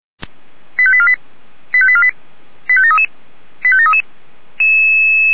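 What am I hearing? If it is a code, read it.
Five-tone selective call (ZVEI), heard as quick runs of five tones, tones of about 70 ms. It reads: 85848 85848 85733 85733
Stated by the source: ZVEI five-tone selective call 89248 received by SDR.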